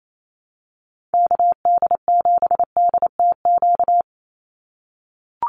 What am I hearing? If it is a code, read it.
Morse code, heard as a keyed tone, about 28 wpm, 700 Hz. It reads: KD7DTQ
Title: USA Call Signs 28wpm